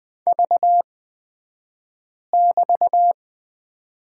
Morse code, heard as a keyed tone, about 20 wpm, spaced as for 5 wpm, 700 Hz. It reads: V<BT>
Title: 223 Lesson 36d - BT - Two Characters 20wpm